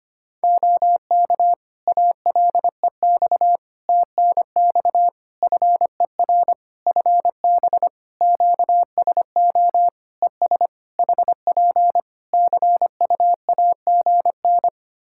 Morse code, heard as a keyed tone, about 25 wpm, 700 Hz. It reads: OK ALEX TNX FER FB QSO ES HP CUAGN
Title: Practice QSO - Exchange 4 - Ending - 25wpm